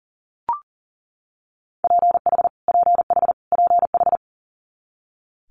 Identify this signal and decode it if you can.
Morse code, heard as a keyed tone, about 40 wpm, 700 Hz. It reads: PH PH PH
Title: Multi-letter Phonemes Repeated 3x 40wpm